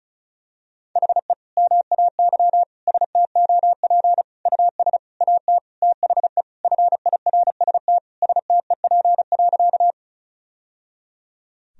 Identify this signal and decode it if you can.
Morse code, heard as a keyed tone, about 36 wpm, 700 Hz. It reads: HE MAY STOP US AT THE FIRST STEP.
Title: Sentences from Top 1000 Words - Review Part 2 - 35wpm